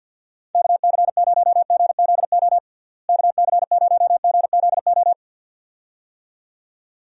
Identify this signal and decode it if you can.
Morse code, heard as a keyed tone, about 48 wpm, 700 Hz. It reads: KC0GZO KC0GZO
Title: USA Call Signs - 6 Characters - Repeated 2x - 50wpm